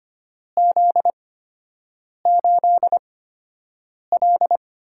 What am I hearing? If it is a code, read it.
Morse code, heard as a keyed tone, about 25 wpm, spaced as for 7 wpm, 700 Hz. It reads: Z8L
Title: USA Call Signs - 8x Character Spacing - 25wpm